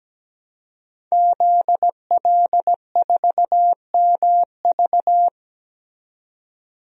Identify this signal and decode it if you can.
Morse code, heard as a keyed tone, about 17 wpm, 700 Hz. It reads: ZL4MV
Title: DXCC-100 Call Signs - 5 Characters - 17wpm